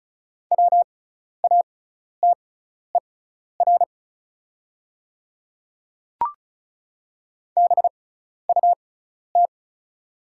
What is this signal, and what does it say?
Morse code, heard as a keyed tone, about 36 wpm, spaced as for 12 wpm, 700 Hz. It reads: WATER BUT
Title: 6x Character Spacing - Top 200 Words - 35wpm